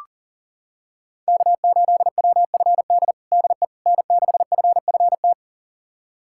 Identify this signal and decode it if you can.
Morse code, heard as a keyed tone, about 40 wpm, 700 Hz. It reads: K8WFD DE N6FFT
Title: USA Callsign DE Callsign - 5 Characters - 40wpm